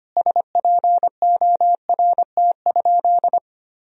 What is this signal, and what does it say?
Morse code, heard as a keyed tone, about 25 wpm, 700 Hz. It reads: SPORT?
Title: Sentences from Top 1000 Words - Review Part 4 - 25wpm